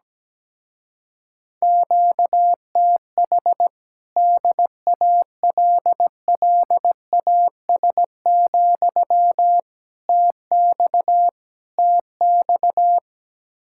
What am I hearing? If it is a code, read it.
Morse code, heard as a keyed tone, about 17 wpm, 700 Hz. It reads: QTH DALLAS, TX TX
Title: Speed-Racing Common QSO Phrases - 17wpm